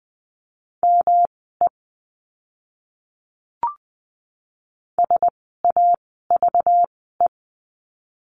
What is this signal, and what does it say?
Morse code, heard as a keyed tone, about 20 wpm, spaced as for 14 wpm, 700 Hz. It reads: ME SAVE
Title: 192 Lesson 31f - Letter v - Words 20wpm